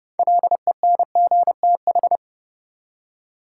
Morse code, heard as a keyed tone, about 31 wpm, 700 Hz. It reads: LENGTH
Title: Six Letter Words - 30wpm